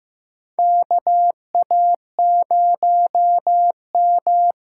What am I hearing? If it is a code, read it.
Morse code, heard as a keyed tone, about 15 wpm, 700 Hz. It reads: KA0M
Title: USA Call Signs - 6 Characters - 8x Character Spacing - 15wpm